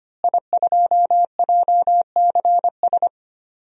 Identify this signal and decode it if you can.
Morse code, heard as a keyed tone, about 25 wpm, 700 Hz. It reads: I2JCS
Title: DXCC-100 Call Signs - 5 Characters - Repeated 2x - 25wpm